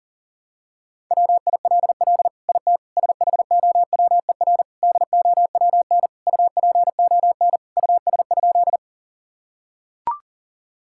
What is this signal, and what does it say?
Morse code, heard as a keyed tone, about 40 wpm, 700 Hz. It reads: WILL IT SHOWER DOWN UPON US?
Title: Sentences from Top 1600 Words 40wpm